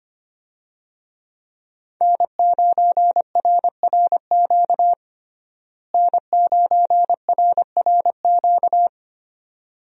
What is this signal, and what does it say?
Morse code, heard as a keyed tone, about 25 wpm, 700 Hz. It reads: N9RRQ N9RRQ
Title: USA Call Signs - 5 Characters - Repeated 2x - 25wpm